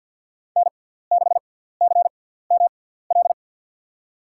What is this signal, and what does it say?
Morse code, heard as a keyed tone, about 50 wpm, spaced as for 17 wpm, 700 Hz. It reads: N6CML
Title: USA Call Signs - 5 Characters - 6x Character Spacing - 50wpm